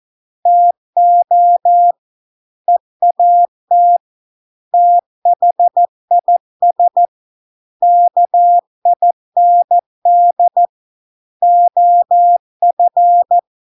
Morse code, heard as a keyed tone, about 14 wpm, 700 Hz. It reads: TO EAT THIS KIND OF